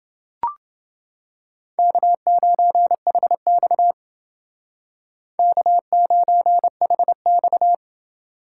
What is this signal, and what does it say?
Morse code, heard as a keyed tone, about 28 wpm, 700 Hz. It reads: K9HX K9HX